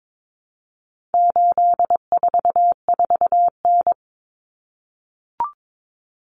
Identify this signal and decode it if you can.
Morse code, heard as a keyed tone, about 22 wpm, 700 Hz. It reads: 844N
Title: Common Cut Numbers - 4-Digit 22wpm